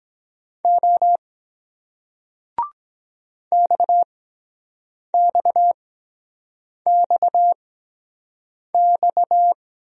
Morse code, heard as a keyed tone, about 23 wpm, spaced as for 7 wpm, 700 Hz. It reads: O XXXX